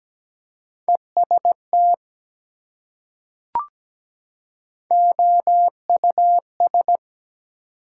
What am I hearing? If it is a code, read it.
Morse code, heard as a keyed tone, about 17 wpm, 700 Hz. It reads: EST OUS
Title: Prefixes and Suffixes 17wpm